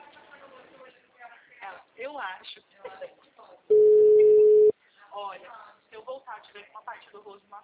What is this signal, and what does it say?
Ringback tone (Europe, 425 Hz), with speech.